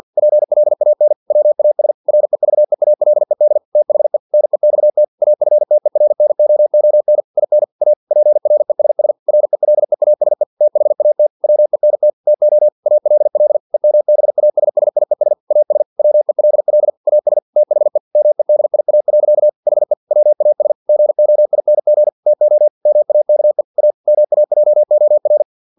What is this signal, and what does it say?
Morse code, heard as a keyed tone, about 50 wpm, 600 Hz. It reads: PLAN WAS REVEALED THE NEXT AFTERNOON IN A PRESS RELEASE THAT WENT TO ALL EMBASSIES AS WELL AS THE MEDIA. HE WAS GOING TO MAKE A MAJOR
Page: Never 49wpm